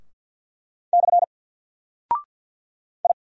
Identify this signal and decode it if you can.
Morse code, heard as a keyed tone, about 48 wpm, 700 Hz. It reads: / I
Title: Single Character (Letter:Number:Punctuation:Prosign) Alt 50wpm